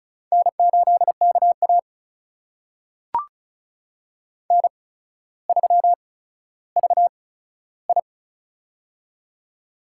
Morse code, heard as a keyed tone, about 36 wpm, spaced as for 15 wpm, 700 Hz. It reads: N8KA N3VI
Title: USA Call Signs - 4 Characters - 8x Character Spacing - 35wpm